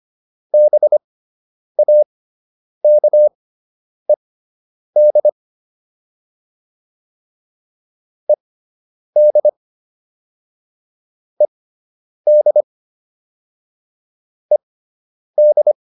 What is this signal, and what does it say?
Morse code, heard as a keyed tone, about 25 wpm, spaced as for 9 wpm, 600 Hz. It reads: BAKED ED ED ED
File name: ED5 09